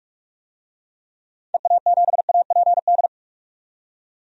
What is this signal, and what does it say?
Morse code, heard as a keyed tone, about 46 wpm, 700 Hz. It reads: EA7APD
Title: DXCC-100 Call Signs - 6 Characters - 45wpm